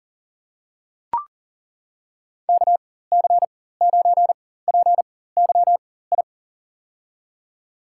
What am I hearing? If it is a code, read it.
Morse code, heard as a keyed tone, about 39 wpm, spaced as for 18 wpm, 700 Hz. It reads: KC9PYI
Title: USA Call Signs - 6 Characters - 4x Character Spacing - 40wpm